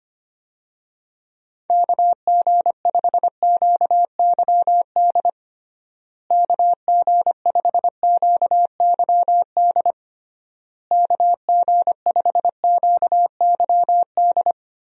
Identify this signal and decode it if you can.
Morse code, heard as a keyed tone, about 25 wpm, 700 Hz. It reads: KG5QYD KG5QYD KG5QYD
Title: USA Call Signs - 6 Characters - Repeated 3x - 25wpm